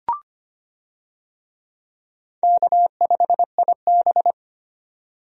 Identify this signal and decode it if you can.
Morse code, heard as a keyed tone, about 25 wpm, 700 Hz. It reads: K5IB
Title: USA Call Signs - 4 Characters - 25wpm